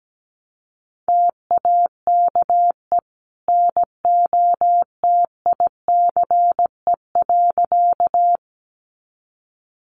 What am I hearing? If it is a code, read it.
Morse code, heard as a keyed tone, about 17 wpm, 700 Hz. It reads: TAKE NOTICE.
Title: Sentences from Top 400 Words 17wpm